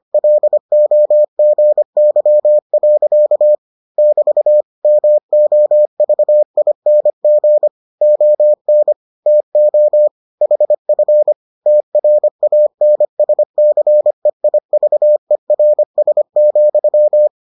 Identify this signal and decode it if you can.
Morse code, heard as a keyed tone, about 25 wpm, 600 Hz. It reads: LOGY. <BT> MOVING ON TO HF TRANSCEIVERS,